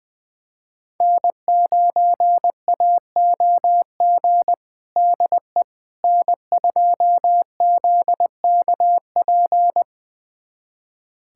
Morse code, heard as a keyed tone, about 20 wpm, 700 Hz. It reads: N9AOG DE N2ZKP
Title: USA Callsign DE Callsign - 5 Characters - 20wpm